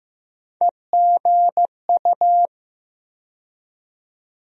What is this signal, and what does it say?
Morse code, heard as a keyed tone, about 15 wpm, 700 Hz. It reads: EGU
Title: USA Call Sign Postfixes - 3 Letters 15wpm